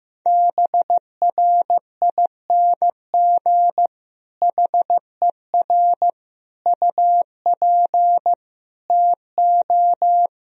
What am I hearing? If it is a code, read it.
Morse code, heard as a keyed tone, about 14 wpm, 700 Hz. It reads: BRING HER UP TO
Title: Sentences from Top 1000 Words - Review Part 3 - 15wpm